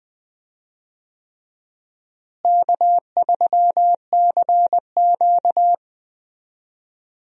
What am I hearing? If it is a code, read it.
Morse code, heard as a keyed tone, about 21 wpm, 700 Hz. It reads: K3CQ
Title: USA Call Signs - 4 Characters - Repeated 3x - 20wpm